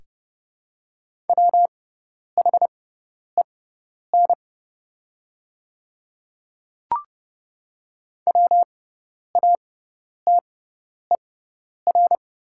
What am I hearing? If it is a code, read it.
Morse code, heard as a keyed tone, about 30 wpm, spaced as for 10 wpm, 700 Hz. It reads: WHEN WATER
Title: 6x Character Spacing - Top 100 Words - 30wpm